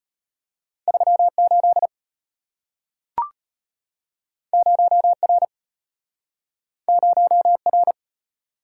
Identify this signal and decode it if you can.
Morse code, heard as a keyed tone, about 34 wpm, 700 Hz. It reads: 38 0R 0R